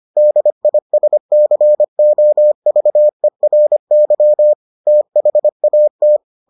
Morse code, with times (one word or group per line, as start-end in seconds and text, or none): 0.2-4.5 DISCOVERY
4.9-6.2 THAT